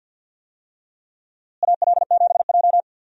Morse code, heard as a keyed tone, about 49 wpm, 700 Hz. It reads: AL7J
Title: USA Call Signs - 4 Characters - Repeated 2x - 50wpm